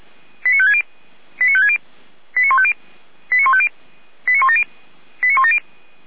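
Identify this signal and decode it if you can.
Five-tone selective call (ZVEI), heard as quick runs of five tones, tones of about 70 ms. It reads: 89560 89560 89150 89150 89170 89170
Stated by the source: ZVEI five-tone call received on an SDR, from Feuerwehr Orlamünde fire brigade.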